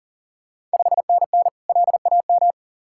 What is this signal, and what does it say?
Morse code, heard as a keyed tone, about 40 wpm, 690 Hz. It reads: 5NN LAM
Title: Ontario QSO Party Exchange 40wpm